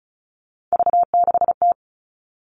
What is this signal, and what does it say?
Morse code, heard as a keyed tone, about 34 wpm, 700 Hz. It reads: V6T
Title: All Cut Numbers - 3-Digit 35wpm